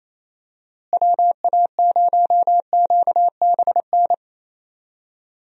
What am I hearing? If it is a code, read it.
Morse code, heard as a keyed tone, about 28 wpm, 700 Hz. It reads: WA0QBN
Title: USA Call Signs - 6 Characters - Repeated 3x - 28wpm